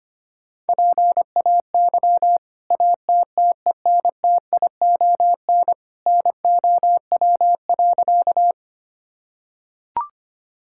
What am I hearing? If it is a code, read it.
Morse code, heard as a keyed tone, about 25 wpm, 700 Hz. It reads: PAY ATTENTION NOW.
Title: Sentences from Top 1000 Words - Review Part 6 - 25wpm